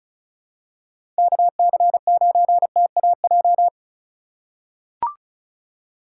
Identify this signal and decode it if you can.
Morse code, heard as a keyed tone, about 36 wpm, 700 Hz. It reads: KC9TAJ
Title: USA Call Signs - 3x Character Spacing - 35wpm